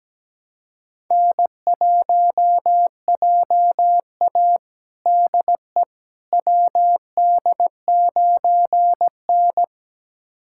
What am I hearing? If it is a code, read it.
Morse code, heard as a keyed tone, about 17 wpm, 700 Hz. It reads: N1JA DE WD9N
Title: USA Callsign DE Callsign - 4 Characters - 17wpm